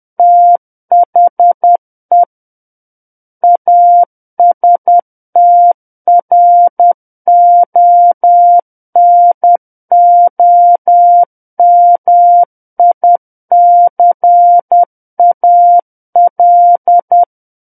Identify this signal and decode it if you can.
Morse code, heard as a keyed tone, about 10 wpm, 700 Hz. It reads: THE ASTRONOMICAL
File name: Book One / 01-Chapter 1 - 10 WPM